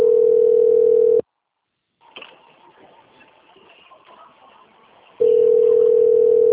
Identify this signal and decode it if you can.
Ringback tone (North America), 440 + 480 Hz.
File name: out- - - - - .2530